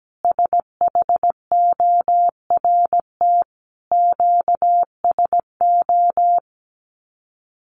Morse code, heard as a keyed tone, about 17 wpm, 700 Hz. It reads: SHORT QSO
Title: Common QSO Phrases 17wpm